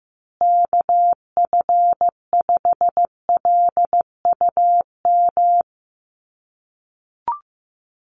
Morse code, heard as a keyed tone, about 15 wpm, 700 Hz. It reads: KF5LUM
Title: USA Call Signs Repeated 2x 15wpm